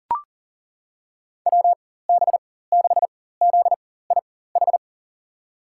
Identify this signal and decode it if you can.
Morse code, heard as a keyed tone, about 38 wpm, spaced as for 18 wpm, 700 Hz. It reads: WB6ZIH
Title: USA Call Signs - 6 Characters - 4x Character Spacing - 40wpm